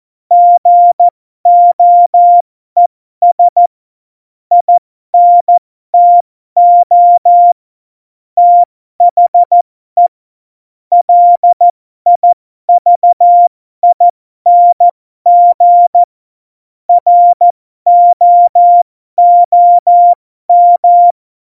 Morse code, heard as a keyed tone, about 14 wpm, spaced as for 12 wpm, 700 Hz. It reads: GOES INTO THE LIVING ROOM
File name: Morsekurs / Witze